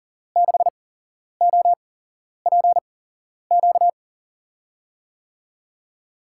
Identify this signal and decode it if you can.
Morse code, heard as a keyed tone, about 40 wpm, spaced as for 11 wpm, 700 Hz. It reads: 6OPQ
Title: USA Call Signs - 6 Characters - 8x Character Spacing - 40wpm